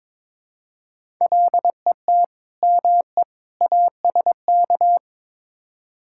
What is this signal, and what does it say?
Morse code, heard as a keyed tone, about 21 wpm, 700 Hz. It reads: LET ME ASK
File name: practice / 22wpm / Build A Phrase - 5-Word Phrases - 22wpm